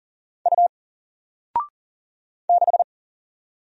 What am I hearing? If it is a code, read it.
Morse code, heard as a keyed tone, about 40 wpm, 700 Hz. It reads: U 6
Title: Single Letter-Number Alt 40wpm